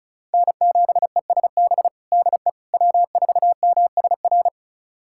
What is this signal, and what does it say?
Morse code, heard as a keyed tone, about 36 wpm, 700 Hz. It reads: N7ESB DE W4MSR
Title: USA Callsign DE Callsign - 5 Characters - 35wpm